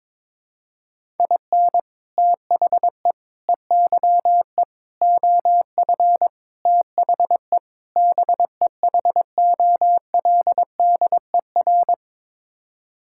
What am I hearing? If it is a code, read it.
Morse code, heard as a keyed tone, about 22 wpm, 700 Hz. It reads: IN THE EYE OF THE BEHOLDER
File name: practice / 22wpm / Sets of 6 Words - 22wpm